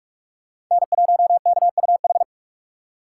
Morse code, heard as a keyed tone, about 41 wpm, 700 Hz. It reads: N1KUH